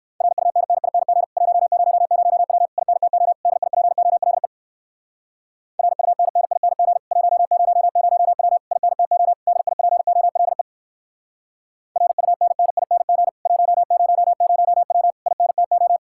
Morse code, heard as a keyed tone, about 59 wpm, 700 Hz. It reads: RUNNING 100W INTO DIPOLE RUNNING 100W INTO DIPOLE RUNNING 100W INTO